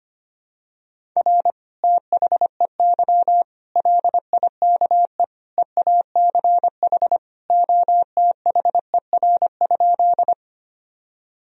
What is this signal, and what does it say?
Morse code, heard as a keyed tone, about 25 wpm, 700 Hz. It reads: R THEY LIKE EACH OTHER?
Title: Sentences from Top 200 Words and Abbreviations 25wpm